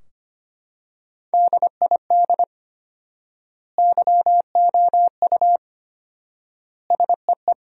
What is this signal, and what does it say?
Morse code, heard as a keyed tone, about 25 wpm, 700 Hz. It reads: DID YOU SEE
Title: 4x Word-Spacing - Sentences from Top 100 Words - 25wpm